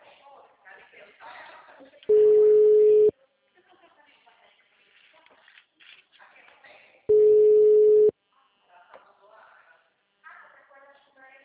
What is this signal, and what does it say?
Ringback tone (Europe, 425 Hz), with speech.
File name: out- -2033- - - .4920